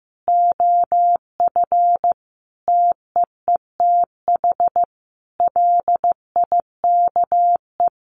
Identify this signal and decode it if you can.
Morse code, heard as a keyed tone, about 15 wpm, 700 Hz. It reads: OF TEETH LIKE
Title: A Sound Of Thunder 15wpm